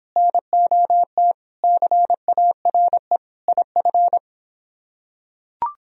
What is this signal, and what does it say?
Morse code, heard as a keyed tone, about 24 wpm, 700 Hz. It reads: NOT CARE IF